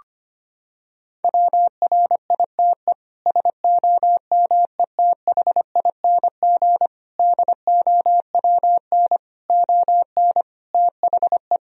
Morse code, heard as a keyed tone, about 25 wpm, 700 Hz. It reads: WRITE SOMETHING DOWN ON THE
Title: Sentences from Top 1000 Words - Review Part 1 - 25wpm